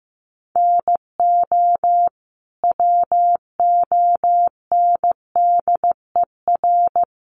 Morse code, heard as a keyed tone, about 15 wpm, 700 Hz. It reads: NO WONDER
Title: Sentences from Top 1100 Words 15wpm